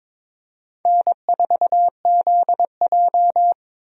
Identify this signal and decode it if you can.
Morse code, heard as a keyed tone, about 22 wpm, 700 Hz. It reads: N4ZJ
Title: USA Call Signs - 4 Characters - 22wpm